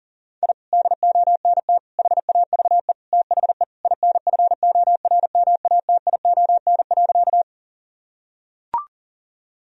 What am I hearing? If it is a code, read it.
Morse code, heard as a keyed tone, about 40 wpm, 700 Hz. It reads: I DONT HAVE THE INFORMATION.
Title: Sentences from Top 1000 Words - Review Part 3 - 40wpm